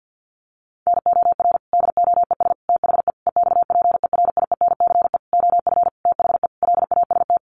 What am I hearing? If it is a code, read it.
Morse code, heard as a keyed tone, about 50 wpm, 700 Hz. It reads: NOR DOES THE EXPERIENCE OF THE LAST
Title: Sentences from Top 1500 Words 50wpm